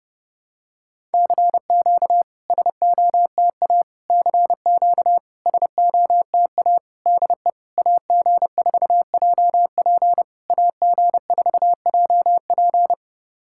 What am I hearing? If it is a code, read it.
Morse code, heard as a keyed tone, about 30 wpm, 700 Hz. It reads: CQ SOTA CQ SOTA DE AG4JP AG4JP
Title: SOTA QSO Practice 30wpm